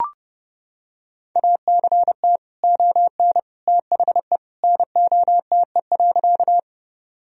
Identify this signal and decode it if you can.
Morse code, heard as a keyed tone, about 30 wpm, 700 Hz. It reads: ACT ON THE NOTE.
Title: Sentences from Top 1000 Words - Review Part 2 - 30wpm